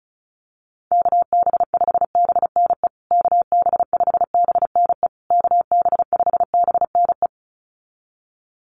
Morse code, heard as a keyed tone, about 34 wpm, 700 Hz. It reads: KB5BNE KB5BNE KB5BNE
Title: USA Call Signs Repeated 3x 35wpm